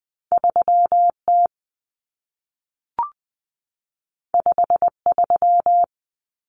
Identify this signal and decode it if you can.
Morse code, heard as a keyed tone, about 20 wpm, 700 Hz. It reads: 3T 53
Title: Common Cut Numbers - 2-Digit 20wpm